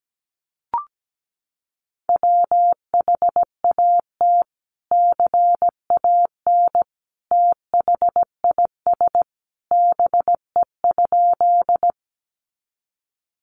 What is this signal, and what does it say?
Morse code, heard as a keyed tone, about 17 wpm, 700 Hz. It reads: WHAT CAN THIS BE?
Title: Sentences from Top 100 Words 17wpm